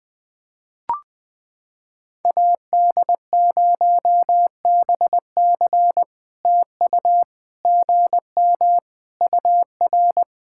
Morse code, heard as a keyed tone, about 20 wpm, 700 Hz. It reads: AD0BC TU GM UR
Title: POTA QSO Practice - 20wpm